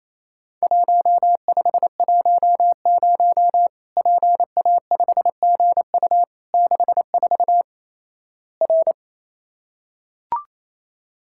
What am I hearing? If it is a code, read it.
Morse code, heard as a keyed tone, about 28 wpm, 700 Hz. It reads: 1510 PA5GU 64 R
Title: Worked All Europe DX Contest - QTC Exchange - 28wpm